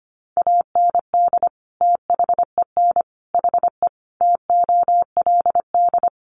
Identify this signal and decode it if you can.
Morse code, heard as a keyed tone, about 25 wpm, 700 Hz. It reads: AND THEN HE TOLD
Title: Sentences from Top 400 Words 25wpm